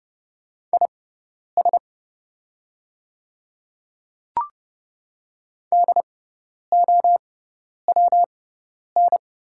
Morse code, heard as a keyed tone, about 30 wpm, spaced as for 10 wpm, 700 Hz. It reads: IS DOWN
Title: 6x Character Spacing - Top 100 Words - 30wpm